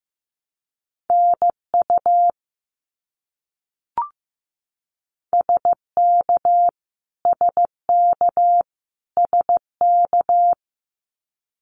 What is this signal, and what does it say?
Morse code, heard as a keyed tone, about 15 wpm, 700 Hz. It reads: NU SK SK SK
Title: Canadian Provinces and Territories - Repeated 3x - 15wpm